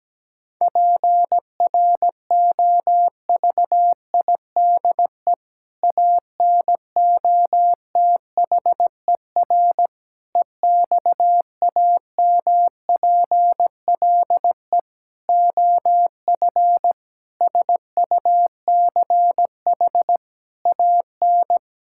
Morse code, heard as a keyed tone, about 17 wpm, 700 Hz. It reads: PROVIDE ANOTHER EXAMPLE OF SUCH AN
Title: Sentences from Top 1000 Words - Review Part 5 - 17wpm